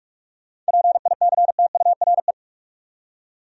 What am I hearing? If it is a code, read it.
Morse code, heard as a keyed tone, about 65 wpm, spaced as for 55 wpm, 700 Hz. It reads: PICTURE
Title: Speed-Racing Top 200 Words 45wpm